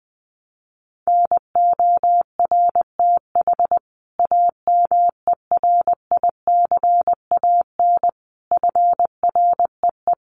Morse code, heard as a keyed tone, about 20 wpm, 700 Hz. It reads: NORTH AMERICAN FREE
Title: Sets of 5 Words Encore 20wpm